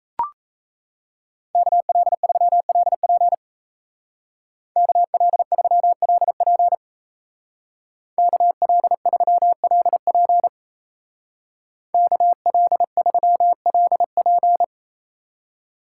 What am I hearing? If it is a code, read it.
Morse code, heard as a keyed tone, about 33 wpm, 700 Hz. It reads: KL3LP KL3LP KL3LP KL3LP
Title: Speed Racing - USA 2x2 Call Signs 28wpm